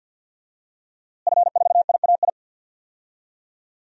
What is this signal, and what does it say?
Morse code, heard as a keyed tone, about 49 wpm, 700 Hz. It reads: F4IAI
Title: DXCC-100 Call Signs - 5 Characters - Repeated 3x - 50wpm